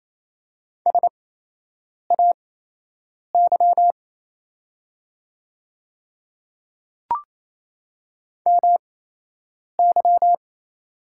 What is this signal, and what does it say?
Morse code, heard as a keyed tone, about 28 wpm, spaced as for 8 wpm, 700 Hz. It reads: SAY MY
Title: 8x Character Spacing - Top 200 Words - 28wpm